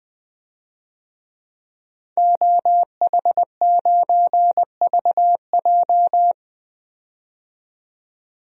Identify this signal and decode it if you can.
Morse code, heard as a keyed tone, about 20 wpm, 700 Hz. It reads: OH9VJ
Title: DXCC-100 Call Signs - 5 Characters - 20wpm